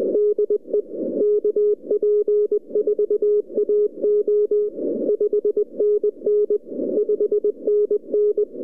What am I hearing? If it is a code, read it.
Morse code, heard as a keyed tone, about 20 wpm, 410 Hz. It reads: DE KP4AO 5NN 5NN
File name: ljud / dl9kr-kp4ao